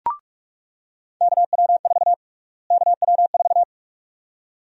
Morse code, heard as a keyed tone, about 44 wpm, 700 Hz. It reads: KW4 KW4
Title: USA Call Sign Prefixes - Repeated 2x - 45wpm